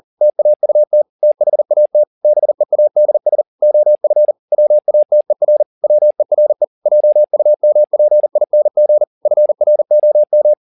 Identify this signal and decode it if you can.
Morse code, heard as a keyed tone, about 40 wpm, 600 Hz. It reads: TAUT THAT BEADS OF WATER WERE JUMPING FROM